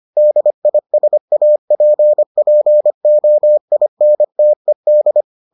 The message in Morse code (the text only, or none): DISAPPOINTED